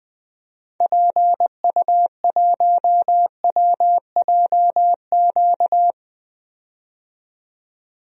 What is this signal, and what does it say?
Morse code, heard as a keyed tone, about 20 wpm, 700 Hz. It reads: PU1WJQ
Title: DXCC-100 Call Signs - 6 Characters - 20wpm